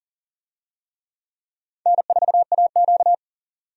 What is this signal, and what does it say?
Morse code, heard as a keyed tone, about 40 wpm, 700 Hz. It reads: N4AQ
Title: USA Call Signs - 4 Characters - 40wpm